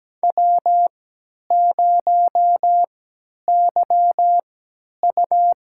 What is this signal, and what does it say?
Morse code, heard as a keyed tone, about 17 wpm, spaced as for 10 wpm, 700 Hz. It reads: W0YU
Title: USA Call Signs - 4 Characters - 3x Character Spacing - 17wpm